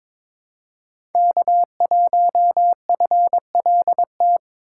Morse code, heard as a keyed tone, about 22 wpm, 700 Hz. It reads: K1FLT
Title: USA Call Signs - 5 Characters - 2x Character Spacing - 22wpm